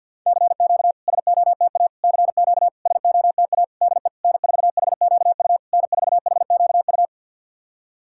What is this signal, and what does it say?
Morse code, heard as a keyed tone, about 51 wpm, 690 Hz. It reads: CQ SOTA CQ SOTA DE N4HQU N4HQU
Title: SOTA QSO Practice 50wpm